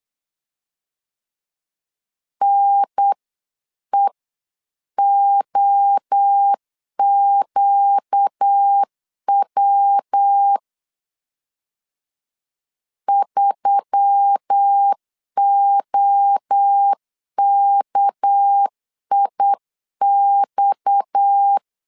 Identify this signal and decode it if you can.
Morse code, heard as a keyed tone, about 9 wpm, 800 Hz. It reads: NEOQW 3OKIX
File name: CW / DET / 1 6 1500C S C 8wpm 5219